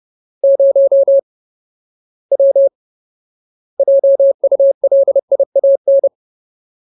Morse code, heard as a keyed tone, about 30 wpm, 550 Hz. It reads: 0 W JULIAN
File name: Ultimate Word List / 30WPM Ultimate